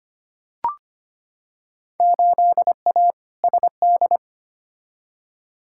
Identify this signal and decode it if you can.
Morse code, heard as a keyed tone, about 25 wpm, 700 Hz. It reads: 8A SD
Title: Field Day Exchanges Encore - 25wpm